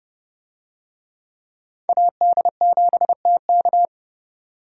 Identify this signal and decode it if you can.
Morse code, heard as a keyed tone, about 30 wpm, 700 Hz. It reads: AD7TK
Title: USA Call Signs - 5 Characters - 30wpm